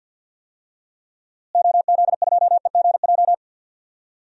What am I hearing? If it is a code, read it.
Morse code, heard as a keyed tone, about 51 wpm, 700 Hz. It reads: OZ2EGJ
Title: DXCC-100 Call Signs - 6 Characters - Repeated 2x - 50wpm